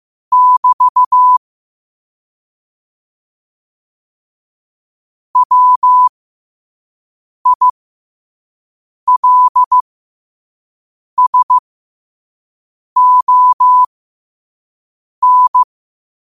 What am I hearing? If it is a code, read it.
Morse code, heard as a keyed tone, about 15 wpm, spaced as for 5 wpm, 1000 Hz. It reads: <BT> WILSON
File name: qotdfiles / 5wpm